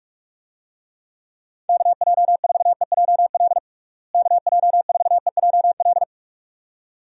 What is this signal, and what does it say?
Morse code, heard as a keyed tone, about 46 wpm, 700 Hz. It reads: KJ4EJL KJ4EJL
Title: USA Call Signs - 6 Characters - Repeated 2x - 45wpm